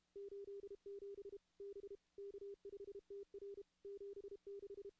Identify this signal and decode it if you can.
Morse code, heard as a keyed tone, about 32 wpm, 400 Hz. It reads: 87 B K5TR 76
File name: contests / CW-SS-k5tr- -2100